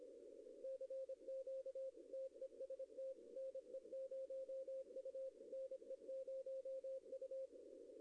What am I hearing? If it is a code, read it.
Morse code, heard as a keyed tone, about 25 wpm, 530 Hz. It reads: CQ TEST NE0U NE0U
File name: contests / arrl-160-k5tr-op-w5tn -0100